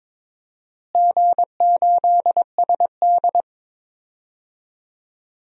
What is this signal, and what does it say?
Morse code, heard as a keyed tone, about 22 wpm, 700 Hz. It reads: G8SD
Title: DXCC-100 Call Signs - 4 Characters - Repeated 2x - 22wpm